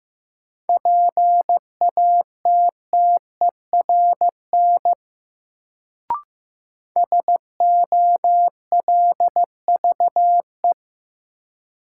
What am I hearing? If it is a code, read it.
Morse code, heard as a keyed tone, about 15 wpm, 700 Hz. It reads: PATTERN SOLVE
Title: Sentences from Top 1000 Words - New Word Drill - 15wpm